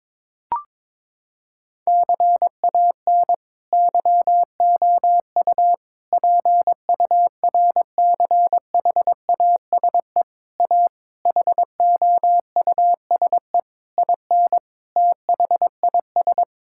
Morse code, heard as a keyed tone, about 22 wpm, 700 Hz. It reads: CAN YOU PURCHASE A HOUSE IN THIS
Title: Sentences from Top 1000 Words - Review Part 5 - 22wpm